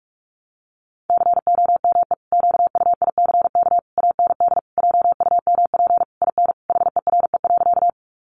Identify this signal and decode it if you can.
Morse code, heard as a keyed tone, about 44 wpm, 700 Hz. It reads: COME QUICK AND JUMP IN HERE.
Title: Sentences from Top 1300 Words 45wpm